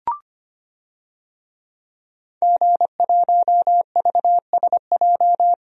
Morse code, heard as a keyed tone, about 25 wpm, 700 Hz. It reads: G1VSJ
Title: DXCC-100 Call Signs - 5 Characters - Repeated 2x - 25wpm